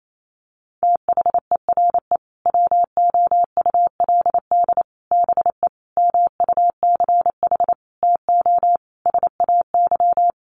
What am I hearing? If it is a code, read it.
Morse code, heard as a keyed tone, about 28 wpm, 700 Hz. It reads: THERE WOULD BE MUCH TO SAY
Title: Sentences from Top 1200 Words 28wpm